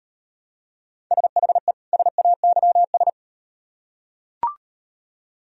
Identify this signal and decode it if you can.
Morse code, heard as a keyed tone, about 36 wpm, 700 Hz. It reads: SHE SAYS